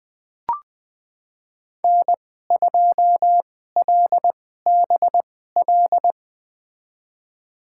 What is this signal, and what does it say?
Morse code, heard as a keyed tone, about 20 wpm, spaced as for 15 wpm, 700 Hz. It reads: N2LBL
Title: USA Call Signs - 5 Characters - 2x Character Spacing - 20wpm